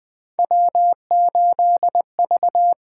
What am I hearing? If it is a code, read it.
Morse code, heard as a keyed tone, about 20 wpm, 700 Hz. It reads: W8V
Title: USA Call Signs - 8x Character Spacing - 20wpm